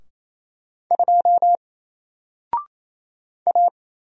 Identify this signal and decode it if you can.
Morse code, heard as a keyed tone, about 27 wpm, 700 Hz. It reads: 2 A
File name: Single-Letter-Number-28wpm